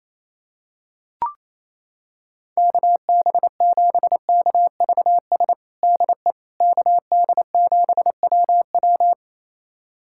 Morse code, heard as a keyed tone, about 28 wpm, 700 Hz. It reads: KB7KVS DE KD7WW
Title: USA Callsign DE Callsign 28wpm